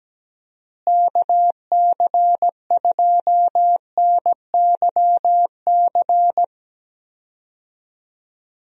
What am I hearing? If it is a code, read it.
Morse code, heard as a keyed tone, about 17 wpm, 700 Hz. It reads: KC2NYC
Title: USA Call Signs - 6 Characters - Repeated 3x - 17wpm